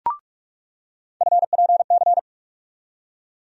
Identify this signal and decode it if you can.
Morse code, heard as a keyed tone, about 46 wpm, 700 Hz. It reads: FPC
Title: USA Call Sign Postfixes - 3 Letters 45wpm